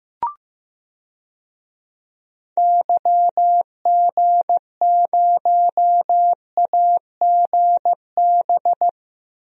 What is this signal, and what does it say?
Morse code, heard as a keyed tone, about 15 wpm, 700 Hz. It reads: YG0AGB
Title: DXCC-100 Call Signs - 6 Characters - 15wpm